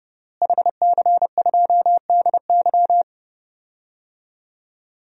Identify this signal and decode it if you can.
Morse code, heard as a keyed tone, about 30 wpm, 700 Hz. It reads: HC2DY
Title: DXCC-100 Call Signs - 5 Characters - Repeated 3x - 30wpm